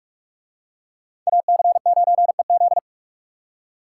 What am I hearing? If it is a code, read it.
Morse code, heard as a keyed tone, about 44 wpm, 700 Hz. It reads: AC9EZ